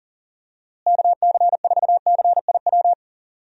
Morse code, heard as a keyed tone, about 40 wpm, 700 Hz. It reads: KC4CIW
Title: USA Call Signs - 6 Characters - Repeated 2x - 40wpm